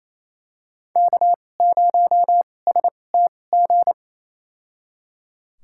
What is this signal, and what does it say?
Morse code, heard as a keyed tone, about 28 wpm, spaced as for 20 wpm, 700 Hz. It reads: K0STG
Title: USA Call Signs - 5 Characters - 2x Character Spacing - 28wpm